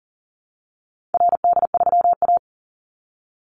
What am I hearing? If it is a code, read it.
Morse code, heard as a keyed tone, about 40 wpm, 700 Hz. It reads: RD3A
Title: Call Signs Repeated 2x - 40wpm